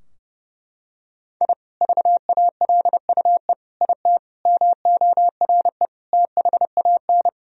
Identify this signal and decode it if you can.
Morse code, heard as a keyed tone, about 30 wpm, 700 Hz. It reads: I VALUE IT MORE THAN
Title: Sentences from Top 1000 Words - Review Part 3 - 30wpm